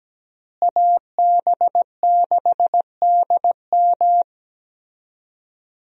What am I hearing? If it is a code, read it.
Morse code, heard as a keyed tone, about 17 wpm, 700 Hz. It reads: AB6DM
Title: USA Call Signs - 5 Characters - Repeated 2x - 17wpm